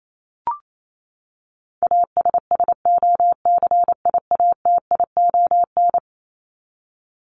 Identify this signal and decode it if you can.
Morse code, heard as a keyed tone, about 28 wpm, 700 Hz. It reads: ASSOCIATION
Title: Top 1001-1250 Words 28wpm